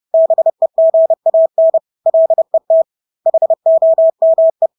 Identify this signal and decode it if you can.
Morse code, heard as a keyed tone, about 30 wpm, 650 Hz. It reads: BEGAN LET HOME
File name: Engelske ord 30-30 wpm